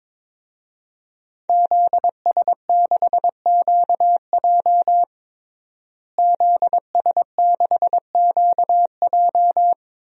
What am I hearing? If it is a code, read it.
Morse code, heard as a keyed tone, about 22 wpm, 700 Hz. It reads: ZS6QJ ZS6QJ
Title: DXCC-100 Call Signs - 5 Characters - Repeated 2x - 22wpm